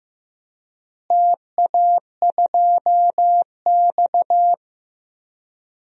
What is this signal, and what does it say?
Morse code, heard as a keyed tone, about 15 wpm, 700 Hz. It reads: TA2X
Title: DXCC-100 Call Signs - 4 Characters - Repeated 3x - 15wpm